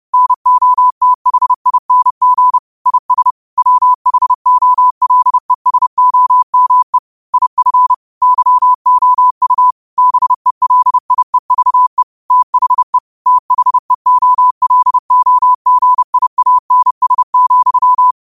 Morse code, heard as a keyed tone, about 30 wpm, 1000 Hz. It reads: NOTHING IS WHOLESOME IF YOU BELIEVE THE THEOLOGIANS,
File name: qotdfiles / 30wpm